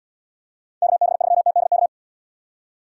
Morse code, heard as a keyed tone, about 76 wpm, 700 Hz. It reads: KB3TDK